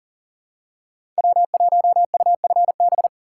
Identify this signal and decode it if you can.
Morse code, heard as a keyed tone, about 40 wpm, 700 Hz. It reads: W1UFB
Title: USA Call Signs - 5 Characters - 4x Character Spacing - 40wpm